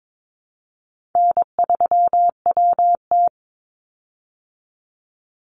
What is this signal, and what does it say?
Morse code, heard as a keyed tone, about 22 wpm, 700 Hz. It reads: N3WT
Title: Easy Call Signs Repeated 2x - 22wpm